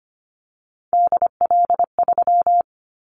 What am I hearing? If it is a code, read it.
Morse code, heard as a keyed tone, about 25 wpm, 700 Hz. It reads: DL3
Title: Callsign Prefixes for Top 50 DXCC - 25wpm